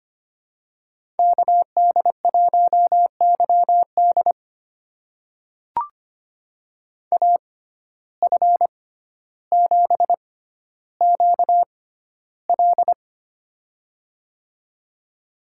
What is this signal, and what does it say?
Morse code, heard as a keyed tone, about 25 wpm, spaced as for 13 wpm, 700 Hz. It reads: KD1YD AF7QL
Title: USA Call Signs - 5 Characters - 6x Character Spacing - 25wpm